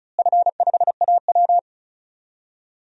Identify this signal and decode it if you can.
Morse code, heard as a keyed tone, about 34 wpm, 700 Hz. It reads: F5AW
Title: DXCC-100 Call Signs - 4 Characters - Repeated 2x - 35wpm